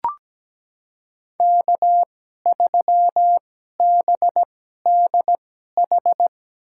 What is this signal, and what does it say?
Morse code, heard as a keyed tone, about 17 wpm, spaced as for 12 wpm, 700 Hz. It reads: K3BDH
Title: USA Call Signs - 2x Character Spacing - 17wpm